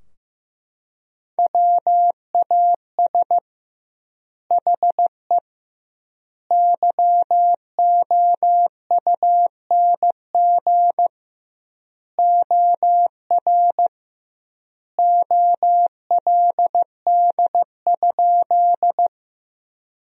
Morse code, heard as a keyed tone, about 15 wpm, 700 Hz. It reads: WAS HE YOUNG OR OLD?
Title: 2x Word-Spacing - Sentences from Top 400 Words - 15wpm